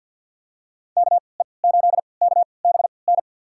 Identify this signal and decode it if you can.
Morse code, heard as a keyed tone, about 53 wpm, spaced as for 29 wpm, 700 Hz. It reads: KE8KBN
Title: USA Call Signs - 3x Character Spacing - 50wpm